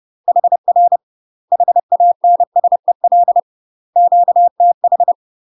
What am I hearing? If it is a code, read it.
Morse code, heard as a keyed tone, about 30 wpm, 700 Hz. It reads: HR HANSEL QTH
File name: UK MASTER QSOs TRAINING/30WPM